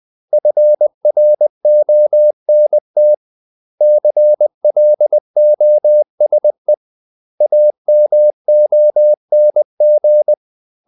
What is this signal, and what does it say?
Morse code, heard as a keyed tone, about 20 wpm, 600 Hz. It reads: FRONT CLOSE AMONG
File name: CW / Practice / 500MostCommonEnglishWords20